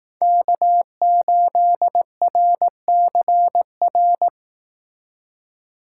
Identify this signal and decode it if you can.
Morse code, heard as a keyed tone, about 18 wpm, 700 Hz. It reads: K8RCR
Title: Speed Racing - USA 1x3 Call Signs 15wpm